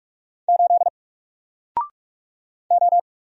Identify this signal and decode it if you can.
Morse code, heard as a keyed tone, about 46 wpm, 700 Hz. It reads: 8 O
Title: Single Letter-Number 45wpm